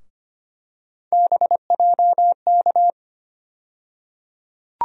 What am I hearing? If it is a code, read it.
Morse code, heard as a keyed tone, about 25 wpm, 700 Hz. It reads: BJK
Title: USA Call Sign Postfixes - 3 Letters 25wpm